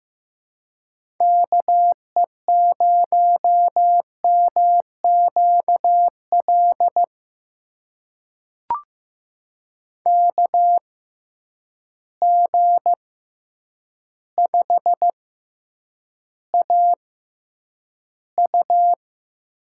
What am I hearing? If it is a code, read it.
Morse code, heard as a keyed tone, about 15 wpm, spaced as for 8 wpm, 700 Hz. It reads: KE0MQL KG5AU
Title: USA Call Signs - 6x Character Spacing - 15wpm